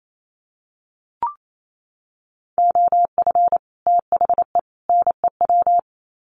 Morse code, heard as a keyed tone, about 28 wpm, 700 Hz. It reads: OF THE NEW
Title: Sets of 3 Words 28wpm